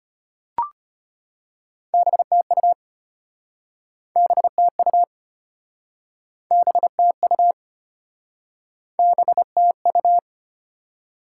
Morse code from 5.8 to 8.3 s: BTU

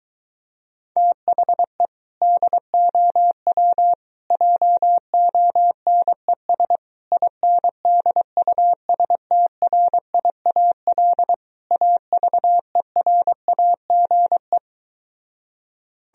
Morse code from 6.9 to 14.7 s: INDUSTRIAL AVERAGE

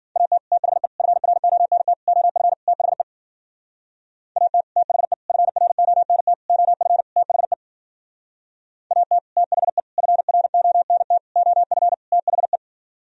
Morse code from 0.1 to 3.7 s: AT THE FRONT OF THE

